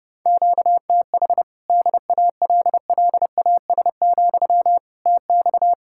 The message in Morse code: QTH DALLAS, TX

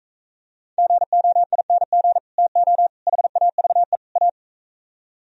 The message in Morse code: GOING TO HAVE A

